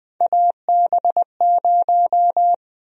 Morse code, heard as a keyed tone, about 20 wpm, 700 Hz. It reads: AB0
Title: USA Call Sign Prefixes - Repeated 2x - 20wpm